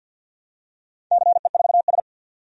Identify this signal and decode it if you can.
Morse code, heard as a keyed tone, about 51 wpm, 700 Hz. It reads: CE4S
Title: DXCC-100 Call Signs - 4 Characters - 50wpm